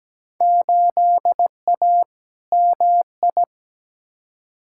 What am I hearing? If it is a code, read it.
Morse code, heard as a keyed tone, about 17 wpm, 700 Hz. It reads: 8A MI
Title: Field Day Exchanges Encore - 17wpm